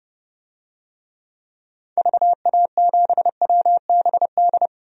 Morse code, heard as a keyed tone, about 30 wpm, 690 Hz. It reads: VA7WBD
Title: DXCC-100 Call Signs - 6 Characters - 30wpm